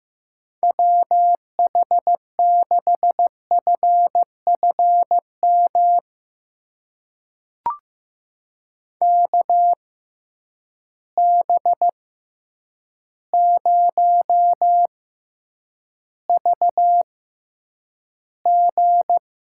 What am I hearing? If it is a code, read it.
Morse code, heard as a keyed tone, about 15 wpm, spaced as for 8 wpm, 700 Hz. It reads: WH6FFM KB0VG
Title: USA Call Signs - 6 Characters - 6x Character Spacing - 15wpm